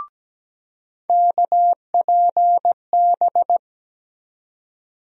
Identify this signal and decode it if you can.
Morse code, heard as a keyed tone, about 17 wpm, 700 Hz. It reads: KPB
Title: USA Call Sign Postfixes - 3 Letters 17wpm